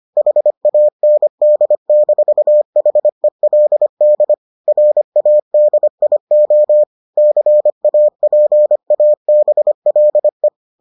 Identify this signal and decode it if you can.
Morse code, heard as a keyed tone, about 25 wpm, 600 Hz. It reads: HAND-HELD RADIO CAPABLE